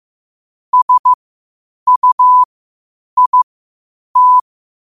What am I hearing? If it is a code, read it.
Morse code, heard as a keyed tone, about 15 wpm, spaced as for 8 wpm, 1000 Hz. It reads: SUIT
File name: qotdfiles / 8wpm